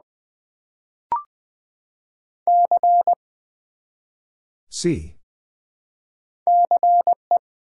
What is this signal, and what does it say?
Morse code, heard as a keyed tone, about 20 wpm, 700 Hz. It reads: C CE
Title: Build A Word - 1 Letter At A Time - 3 to 5-Letter Words - 20wpm